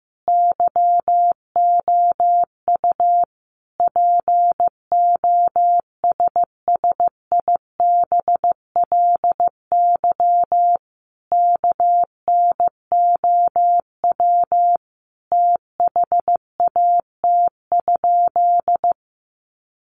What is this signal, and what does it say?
Morse code, heard as a keyed tone, about 15 wpm, 700 Hz. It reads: YOU POSSIBLY KNOW THAT?
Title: Sentences from Top 1300 Words 15wpm